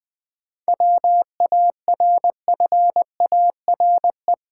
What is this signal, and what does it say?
Morse code, heard as a keyed tone, about 20 wpm, 700 Hz. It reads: WARFARE
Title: 7 Letter Words - First 3 Letters Given - 20wpm